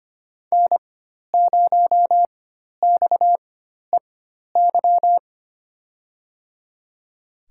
Morse code, heard as a keyed tone, about 25 wpm, spaced as for 12 wpm, 700 Hz. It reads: N0XEY
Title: USA Call Signs - 5 Characters - 4x Character Spacing - 25wpm